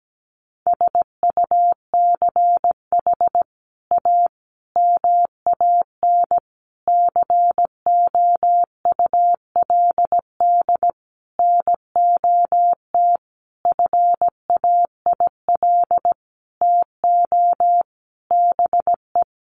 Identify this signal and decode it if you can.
Morse code, heard as a keyed tone, about 17 wpm, 700 Hz. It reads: SUCH A MAN COULD NOT FAIL TO BE
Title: Sentences from Top 1100 Words 17wpm